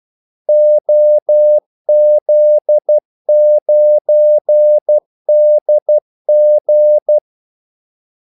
Morse code, heard as a keyed tone, about 12 wpm, 600 Hz. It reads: OZ9DG
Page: Kaldesignaler 12 wpm